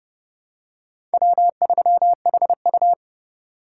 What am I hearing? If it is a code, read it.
Morse code, heard as a keyed tone, about 31 wpm, 700 Hz. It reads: W3HU